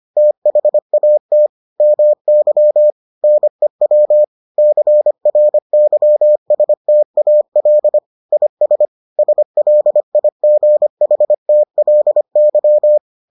Morse code, heard as a keyed tone, about 25 wpm, 600 Hz. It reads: THAT MY NEW CRYSTAL IS SLIGHTLY